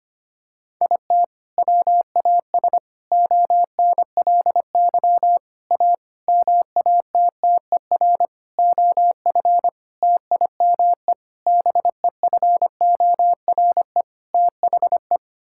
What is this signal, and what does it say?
Morse code, heard as a keyed tone, about 25 wpm, 700 Hz. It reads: IT WAS ONLY A MATTER OF TIME BEFORE THE
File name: Sets of 9 Words - 25wpm